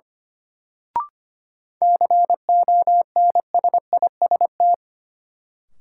Morse code, heard as a keyed tone, about 25 wpm, 700 Hz. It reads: CONSIST
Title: Sentences from Top 1400 Words - New Word Drill Alt - 25wpm